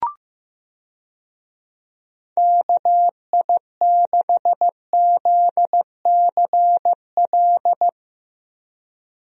Morse code, heard as a keyed tone, about 15 wpm, 700 Hz. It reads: KI6ZCL
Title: USA Call Signs - 6 Characters - Repeated 2x - 15wpm